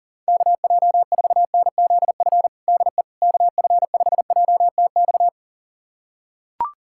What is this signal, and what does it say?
Morse code, heard as a keyed tone, about 40 wpm, 700 Hz. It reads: KJ4NZF DE KF5JTX
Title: USA Callsign DE Callsign - 6 Characters - 40wpm